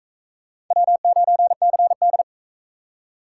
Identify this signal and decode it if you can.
Morse code, heard as a keyed tone, about 38 wpm, 700 Hz. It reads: W9CD